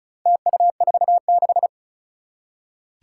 Morse code, heard as a keyed tone, about 36 wpm, 700 Hz. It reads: TU46
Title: Four Challenging Letter-Number Combinations 35wpm